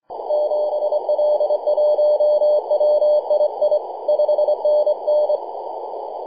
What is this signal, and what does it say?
Morse code, heard as a keyed tone, about 24 wpm, 600 Hz, with noise. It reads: 7L1WII 5NN
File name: e6zs-21-009mhz-cw